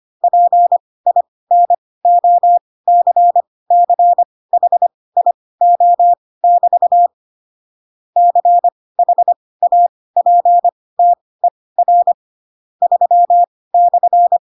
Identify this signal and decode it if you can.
Morse code, heard as a keyed tone, about 25 wpm, spaced as for 18 wpm, 700 Hz. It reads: PINOCCHIO<BT> CHAPTER 3/